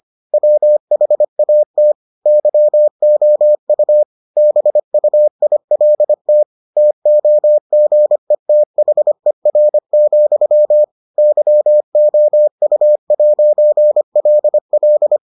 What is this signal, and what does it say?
Morse code, heard as a keyed tone, about 25 wpm, 600 Hz. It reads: WHAT YOU BUILT TOGETHER, YOU'LL